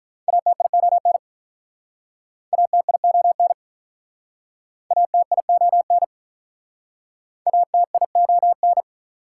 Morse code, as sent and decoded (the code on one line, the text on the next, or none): .- - .. --- -. / .- - .. --- -. / .- - .. --- -. / .- - .. --- -.
ATION ATION ATION ATION